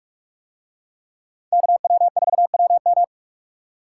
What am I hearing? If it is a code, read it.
Morse code, heard as a keyed tone, about 42 wpm, 700 Hz. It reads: KW4WM